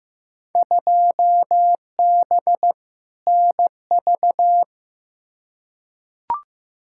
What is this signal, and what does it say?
Morse code, heard as a keyed tone, about 15 wpm, 700 Hz. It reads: 2B NV
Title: Field Day Exchanges Encore - 15wpm